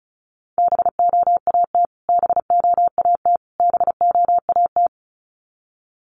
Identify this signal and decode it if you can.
Morse code, heard as a keyed tone, about 34 wpm, 700 Hz. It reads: BOAT BOAT BOAT
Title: Top 500 Words Repeated 35wpm